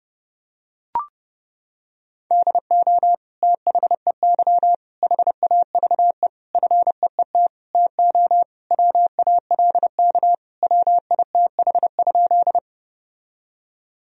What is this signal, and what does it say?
Morse code, heard as a keyed tone, about 30 wpm, 700 Hz. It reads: DO THEY HAVE FEET TO WALK WITH?
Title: Sentences from Top 1000 Words - Review Part 1 - 30wpm